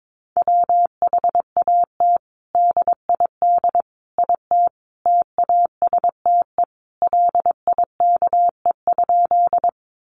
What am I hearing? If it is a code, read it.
Morse code, heard as a keyed tone, about 22 wpm, 700 Hz. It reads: WHAT DID IT TASTE LIKE?
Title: Sentences from Top 1100 Words 22wpm